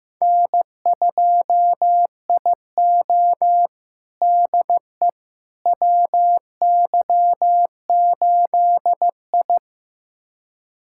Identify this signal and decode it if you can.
Morse code, heard as a keyed tone, about 15 wpm, 700 Hz. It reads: N2IO DE WY8I
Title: USA Callsign DE Callsign - 4 Characters - 15wpm